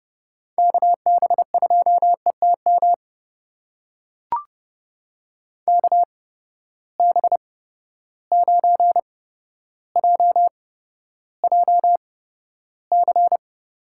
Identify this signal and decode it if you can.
Morse code, heard as a keyed tone, about 30 wpm, spaced as for 13 wpm, 700 Hz. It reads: KB2ETM KB9JJC
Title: USA Call Signs - 6 Characters - 8x Character Spacing - 30wpm